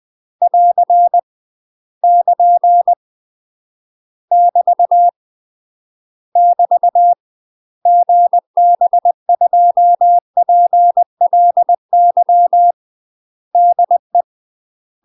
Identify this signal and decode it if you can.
Morse code, heard as a keyed tone, about 20 wpm, 700 Hz. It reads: <AR> <KN> <BT> <BT> GB2PLY DE
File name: 20WPM